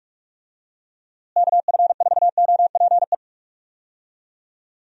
Morse code, heard as a keyed tone, about 46 wpm, 700 Hz. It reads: KF4OPE
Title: USA Call Signs - 6 Characters - Repeated 2x - 45wpm